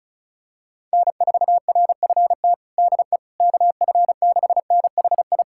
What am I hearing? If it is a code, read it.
Morse code, heard as a keyed tone, about 36 wpm, 700 Hz. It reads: N4RFT DE KF6NHI
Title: USA Callsign DE Callsign 35wpm